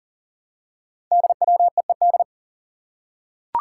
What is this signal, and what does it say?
Morse code, heard as a keyed tone, about 42 wpm, 700 Hz. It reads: DWEED
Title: Code Groups - 5 Letter:Numbers - 40wpm